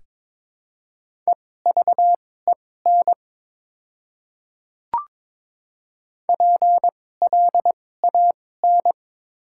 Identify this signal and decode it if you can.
Morse code, heard as a keyed tone, about 22 wpm, spaced as for 16 wpm, 700 Hz. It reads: EVEN PLAN
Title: 2x Character Spacing - Top 500 Words - 22wpm